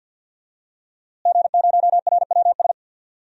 Morse code, heard as a keyed tone, about 47 wpm, 700 Hz. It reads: G0RWS
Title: DXCC-100 Call Signs - 5 Characters - Repeated 2x - 50wpm